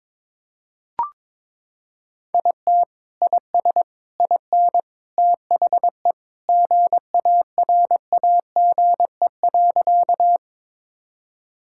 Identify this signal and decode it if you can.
Morse code, heard as a keyed tone, about 22 wpm, 700 Hz. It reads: IT IS IN THE GARAGE.
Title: Sentences from Top 1600 Words 22wpm